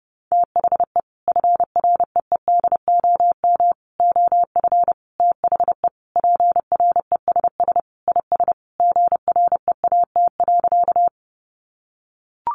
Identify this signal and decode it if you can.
Morse code, heard as a keyed tone, about 30 wpm, 700 Hz. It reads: THE FREEDOM OF THE PRESS IS GREAT.
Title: Sentences from Top 800 Words 30wpm